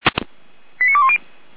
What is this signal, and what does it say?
Five-tone selective call (ZVEI), heard as quick runs of five tones, tones of about 70 ms. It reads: 89210